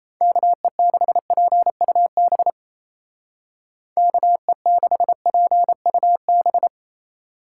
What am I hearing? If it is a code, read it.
Morse code, heard as a keyed tone, about 34 wpm, 700 Hz. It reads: KE6PUB KE6PUB
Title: Speed Racing - USA Call Signs 28wpm